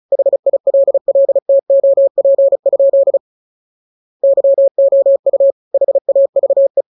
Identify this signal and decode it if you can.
Morse code, heard as a keyed tone, about 34 wpm, 550 Hz. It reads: HILLTOP? YOU HAVE